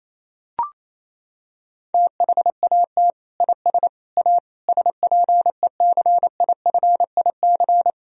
Morse code, heard as a keyed tone, about 28 wpm, 700 Hz. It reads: THAT IS A SPECIFIC
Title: Sentences from Top 1000 Words - Review Part 3 - 28wpm